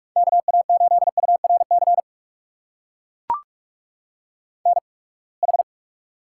Morse code, heard as a keyed tone, about 41 wpm, spaced as for 30 wpm, 700 Hz. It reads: KA8URC NH
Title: USA Call Signs - 8x Character Spacing - 45wpm